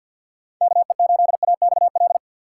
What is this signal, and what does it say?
Morse code, heard as a keyed tone, about 50 wpm, 700 Hz. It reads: KE8AXL
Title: USA Call Signs - 6 Characters - 3x Character Spacing - 50wpm